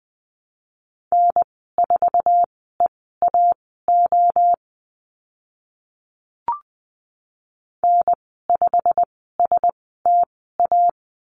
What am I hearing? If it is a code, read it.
Morse code, heard as a keyed tone, about 20 wpm, spaced as for 15 wpm, 700 Hz. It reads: N4EAO N5STA
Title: 088 Lesson 15g - Number 5 - Call Signs 20wpm